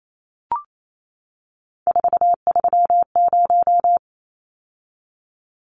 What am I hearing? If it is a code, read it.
Morse code, heard as a keyed tone, about 28 wpm, 700 Hz. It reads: 430
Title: Numbers - 3-Digit 28wpm